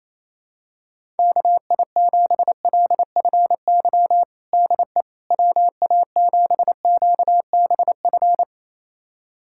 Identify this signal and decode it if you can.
Morse code, heard as a keyed tone, about 28 wpm, 700 Hz. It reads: KI7LFY DE WA7QBF
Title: USA Callsign DE Callsign 28wpm